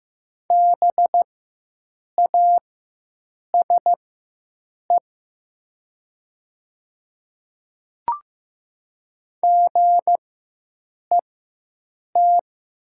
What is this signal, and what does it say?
Morse code, heard as a keyed tone, about 15 wpm, spaced as for 7 wpm, 700 Hz. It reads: BASE GET
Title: 4x Character Spacing - Top 300 Words - 15wpm